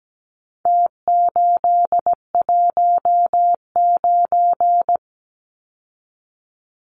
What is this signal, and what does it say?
Morse code, heard as a keyed tone, about 17 wpm, 700 Hz. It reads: T819
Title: Numbers - Contest Style - 4-Digit 17wpm